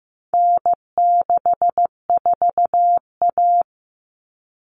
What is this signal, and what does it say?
Morse code, heard as a keyed tone, about 15 wpm, 700 Hz. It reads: N64A
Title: Common Cut Numbers - 4-Digit 15wpm